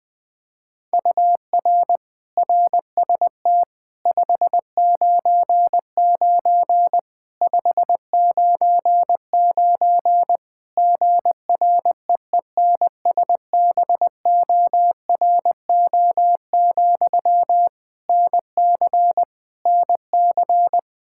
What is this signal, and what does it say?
Morse code, heard as a keyed tone, about 20 wpm, 700 Hz. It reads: UR RST 599 599 GREENSBORO, NC NC
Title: Practice QSO - Exchange 1 - Beginning - 20wpm